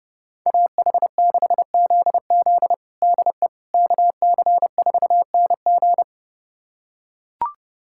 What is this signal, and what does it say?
Morse code, heard as a keyed tone, about 30 wpm, 700 Hz. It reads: AH6ZZ DE KC4NG
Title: USA Callsign DE Callsign - 5 Characters - 30wpm